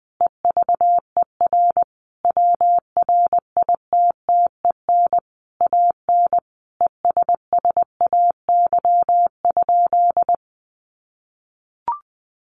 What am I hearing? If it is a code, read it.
Morse code, heard as a keyed tone, about 20 wpm, 700 Hz. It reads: EVER WRITTEN AN ESSAY?
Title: Sentences from Top 1500 Words 20wpm